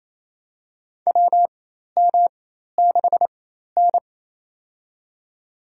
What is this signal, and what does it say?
Morse code, heard as a keyed tone, about 27 wpm, spaced as for 13 wpm, 700 Hz. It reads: WM6N
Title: USA Call Signs - 4 Characters - 4x Character Spacing - 28wpm